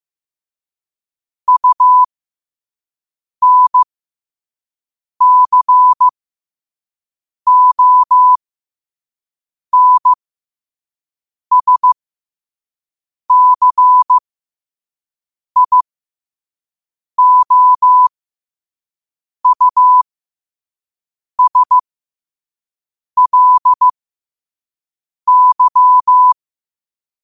Morse code, heard as a keyed tone, about 15 wpm, spaced as for 5 wpm, 1000 Hz. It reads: UNCONSCIOUSLY
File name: qotdfiles / 5wpm